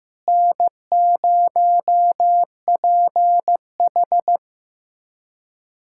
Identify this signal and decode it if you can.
Morse code, heard as a keyed tone, about 15 wpm, 700 Hz. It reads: N0PH
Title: USA Callsign DE Callsign - 4 Characters - 15wpm